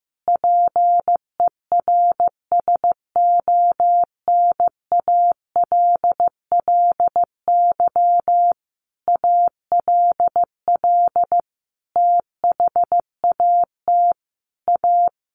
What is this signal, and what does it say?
Morse code, heard as a keyed tone, about 15 wpm, 700 Hz. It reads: PERSONALLY ALL THAT A
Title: Sentences from Top 1400 Words 15wpm